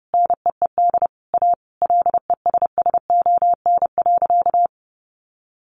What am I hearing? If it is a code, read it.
Morse code, heard as a keyed tone, about 30 wpm, 700 Hz. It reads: NEED A LESSON.
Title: Sentences from Top 1200 Words 30wpm